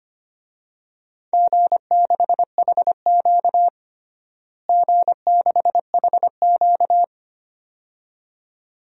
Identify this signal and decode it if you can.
Morse code, heard as a keyed tone, about 25 wpm, 700 Hz. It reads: G6HQ G6HQ
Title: DXCC-100 Call Signs - 4 Characters - Repeated 2x - 25wpm